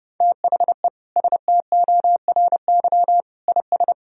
Morse code, heard as a keyed tone, about 32 wpm, 700 Hz. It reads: THE STORY IS
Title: Sentences from Top 1000 Words - Review Part 4 - 30wpm